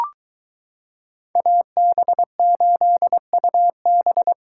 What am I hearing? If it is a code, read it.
Morse code, heard as a keyed tone, about 22 wpm, 700 Hz. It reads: AB8UB